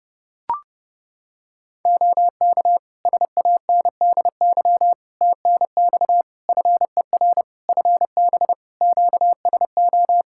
Morse code, heard as a keyed tone, about 28 wpm, 700 Hz. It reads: OK SANDY TNX FER FB QSO
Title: K1USN SST QSO Practice 30wpm